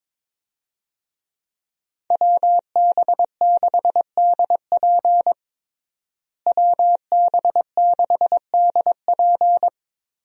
Morse code, heard as a keyed tone, about 22 wpm, 700 Hz. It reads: WB6DP WB6DP
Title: USA Call Signs - 5 Characters - Repeated 3x - 22wpm